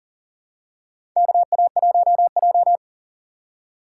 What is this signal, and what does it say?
Morse code, heard as a keyed tone, about 39 wpm, 690 Hz. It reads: KA1J
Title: USA Call Signs - 4x Character Spacing - 40wpm